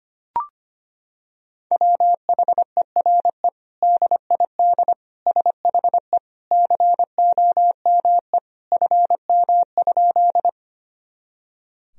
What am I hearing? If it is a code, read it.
Morse code, heard as a keyed tone, about 25 wpm, 700 Hz. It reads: WHERE DID SHE COME FM?
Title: Sentences from Top 200 Words and Abbreviations 25wpm